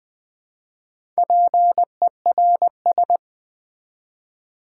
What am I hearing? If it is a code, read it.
Morse code, heard as a keyed tone, about 20 wpm, 700 Hz. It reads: PERS
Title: Build A Word - 4 Letters At A Time - 10-Letter Words - 20wpm